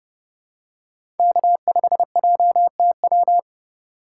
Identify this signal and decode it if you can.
Morse code, heard as a keyed tone, about 29 wpm, 700 Hz. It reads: K5JTW